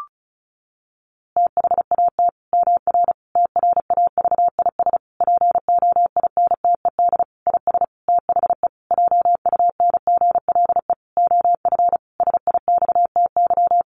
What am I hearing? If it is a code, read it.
Morse code, heard as a keyed tone, about 34 wpm, 700 Hz. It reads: THAT MR TRAVIS POINTED IS THE JUNGLE OF SIXTY
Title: A Sound Of Thunder 35wpm